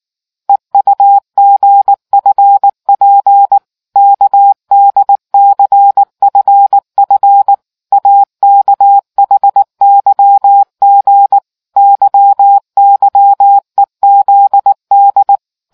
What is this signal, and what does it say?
Morse code, heard as a keyed tone, about 19 wpm, 800 Hz. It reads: EUGFP KDCFF AKHYG YYEZD